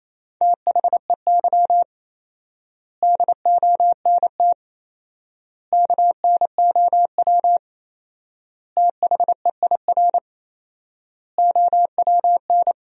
Morse code, heard as a keyed tone, about 28 wpm, 700 Hz. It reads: THEY DONT KNOW THEIR OWN
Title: 4x Word-Spacing - Sentences from Top 300 Words - 28wpm